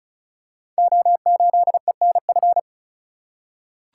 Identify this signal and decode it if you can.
Morse code, heard as a keyed tone, about 35 wpm, 700 Hz. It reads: O8ENF
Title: Code Groups - 5 Letter:Numbers - 35wpm